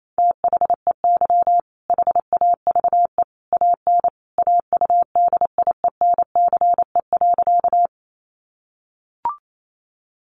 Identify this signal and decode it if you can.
Morse code, heard as a keyed tone, about 28 wpm, 700 Hz. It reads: THEY HAVE AN AUDIENCE.
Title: Sentences from Top 800 Words 28wpm